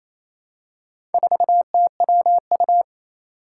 Morse code, heard as a keyed tone, about 28 wpm, 700 Hz. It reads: 4TWU
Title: Four Challenging Letter-Number Combinations 28wpm